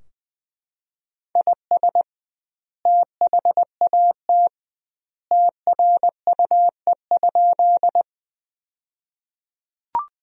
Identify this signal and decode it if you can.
Morse code, heard as a keyed tone, about 20 wpm, 700 Hz. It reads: IS THAT TRUE?
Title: 2x Word-Spacing - Sentences from Top 400 Words - 20wpm